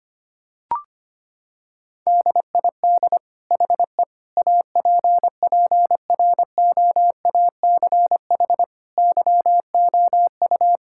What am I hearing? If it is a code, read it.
Morse code, heard as a keyed tone, about 25 wpm, 700 Hz. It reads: DID HE APPROACH YOU
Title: Sentences from Top 1000 Words - Review Part 6 - 25wpm